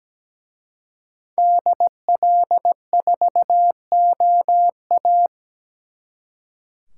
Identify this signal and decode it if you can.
Morse code, heard as a keyed tone, about 20 wpm, 700 Hz. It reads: DL4OA
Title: Speed Racing - USA 2x2 Call Signs 17wpm